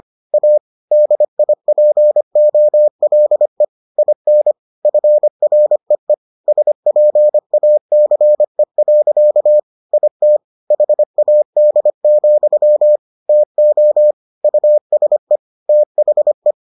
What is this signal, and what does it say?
Morse code, heard as a keyed tone, about 25 wpm, 600 Hz. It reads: A DIPOLE IN FREE SPACE. IT HAD, TO USE THE